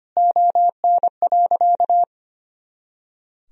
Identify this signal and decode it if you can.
Morse code, heard as a keyed tone, about 25 wpm, 700 Hz. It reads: ON.
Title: Sentences from Top 100 Words and Abbreviations 25wpm